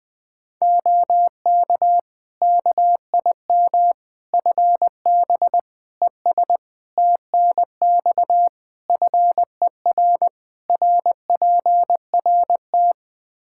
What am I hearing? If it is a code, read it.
Morse code, heard as a keyed tone, about 20 wpm, 700 Hz. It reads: OK KIM FB ES TNX FER RPRT
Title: Practice QSO - Exchange 2 - Info - 20wpm